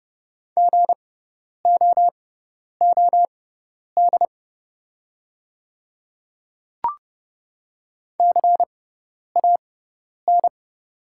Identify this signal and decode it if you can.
Morse code, heard as a keyed tone, about 30 wpm, spaced as for 10 wpm, 700 Hz. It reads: GOOD CAN
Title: 6x Character Spacing - Top 300 Words - 30wpm